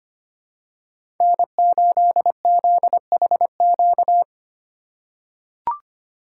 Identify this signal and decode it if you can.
Morse code, heard as a keyed tone, about 25 wpm, 700 Hz. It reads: N8ZHQ
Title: USA Call Signs - 5 Characters - 2x Character Spacing - 25wpm